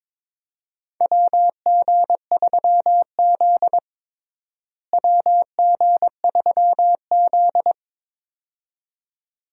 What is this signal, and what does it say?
Morse code, heard as a keyed tone, about 22 wpm, 700 Hz. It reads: WG3Z WG3Z
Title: USA Call Signs - 4 Characters - Repeated 2x - 22wpm